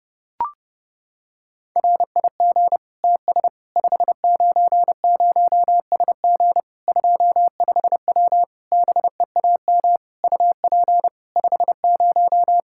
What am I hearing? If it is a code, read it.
Morse code, heard as a keyed tone, about 30 wpm, 700 Hz. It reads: RIG TS 590SG 25W BEAM UP 50
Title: Practice QSO - Exchange 2 - Info - 30wpm